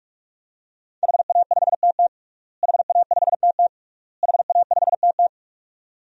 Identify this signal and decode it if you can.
Morse code, heard as a keyed tone, about 48 wpm, 700 Hz. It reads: HA5TT HA5TT HA5TT
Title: DXCC-100 Call Signs - 5 Characters - Repeated 3x - 45wpm